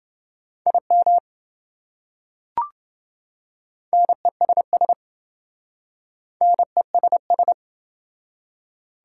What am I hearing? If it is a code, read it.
Morse code, heard as a keyed tone, about 26 wpm, 700 Hz. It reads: IM NESS NESS